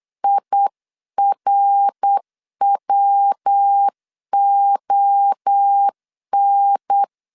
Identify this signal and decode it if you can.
Morse code, heard as a keyed tone, about 8 wpm, 800 Hz. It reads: IRWON